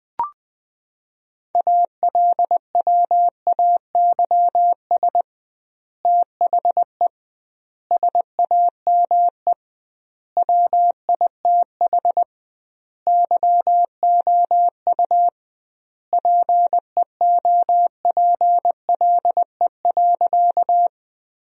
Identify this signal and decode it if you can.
Morse code, heard as a keyed tone, about 20 wpm, 700 Hz. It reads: ALWAYS THE SAME WITH YOU PEOPLE.
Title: 2x Word-Spacing - Sentences from Top 300 Words - 20wpm